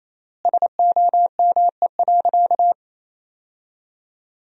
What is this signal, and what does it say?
Morse code, heard as a keyed tone, about 28 wpm, 700 Hz. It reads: SOME.
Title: 3x Word-Spacing - Sentences from Top 100 Words - 28wpm